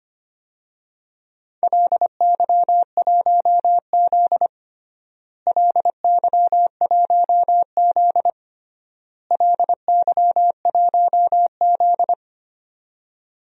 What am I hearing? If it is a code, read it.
Morse code, heard as a keyed tone, about 25 wpm, 700 Hz. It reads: LY1Z LY1Z LY1Z
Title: DXCC-100 Call Signs - 4 Characters - Repeated 3x - 25wpm